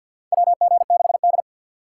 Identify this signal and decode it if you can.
Morse code, heard as a keyed tone, about 46 wpm, 700 Hz. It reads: WG6D
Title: USA Call Signs - 2x Character Spacing - 50wpm